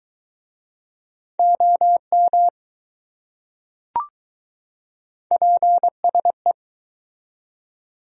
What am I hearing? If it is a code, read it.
Morse code, heard as a keyed tone, about 20 wpm, 700 Hz. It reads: OM PSE